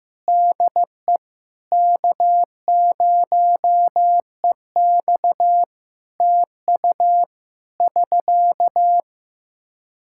Morse code, heard as a keyed tone, about 15 wpm, 700 Hz. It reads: DE K0EX TU <SK>
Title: Practice QSO - Exchange 4 - Ending - 15wpm